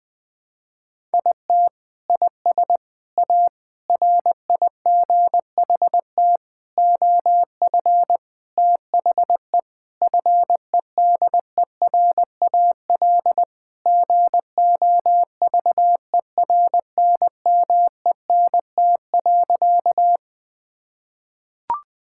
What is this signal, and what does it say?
Morse code, heard as a keyed tone, about 20 wpm, 700 Hz. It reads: IT IS A RIGHT OF THE FEDERAL GOVERNMENT.
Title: Sentences from Top 1000 Words - Review Part 5 - 20wpm